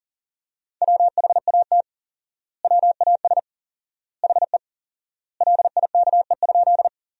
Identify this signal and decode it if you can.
Morse code, heard as a keyed tone, about 40 wpm, 700 Hz. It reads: WHAT WAS HE LIKE?
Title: 4x Word-Spacing - Sentences from Top 100 Words - 40wpm